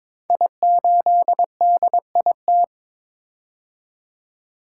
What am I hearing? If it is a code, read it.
Morse code, heard as a keyed tone, about 22 wpm, 690 Hz. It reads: I8DIT
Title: Code Groups - 5 Letter:Numbers - 22wpm